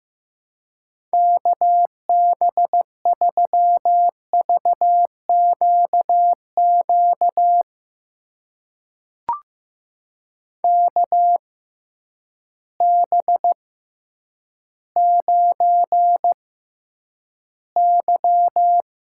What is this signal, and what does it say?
Morse code, heard as a keyed tone, about 15 wpm, spaced as for 9 wpm, 700 Hz. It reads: KB3VQQ KB9Y
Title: USA Call Signs - 6x Character Spacing - 15wpm